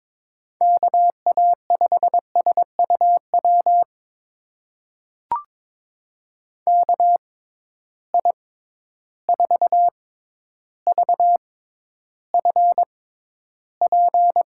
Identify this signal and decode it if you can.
Morse code, heard as a keyed tone, about 22 wpm, spaced as for 11 wpm, 700 Hz. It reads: KA5SUW KI4VFP
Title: USA Call Signs - 6 Characters - 6x Character Spacing - 22wpm